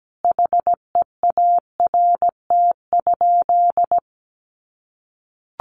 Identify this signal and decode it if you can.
Morse code, heard as a keyed tone, about 17 wpm, 700 Hz. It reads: HEART?
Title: Sentences from Top 1400 Words 17wpm